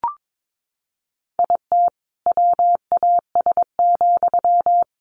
Morse code, heard as a keyed tone, about 22 wpm, 700 Hz. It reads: IT WAS,
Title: Sentences from Top 600 Words 22wpm